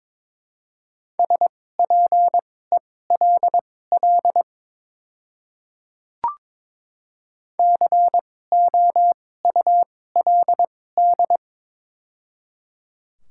Morse code, heard as a keyed tone, about 22 wpm, spaced as for 16 wpm, 700 Hz. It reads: SPELL COULD
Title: 2x Character Spacing - Top 200 Words - 22wpm